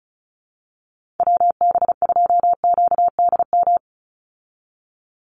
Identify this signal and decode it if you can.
Morse code, heard as a keyed tone, about 34 wpm, 700 Hz. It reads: WB2QDM
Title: USA Call Signs 35wpm